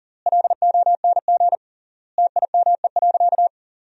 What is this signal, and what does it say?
Morse code, heard as a keyed tone, about 37 wpm, 700 Hz. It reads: LONG TIME.
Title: 3x Word-Spacing - Sentences from Top 400 Words - 40wpm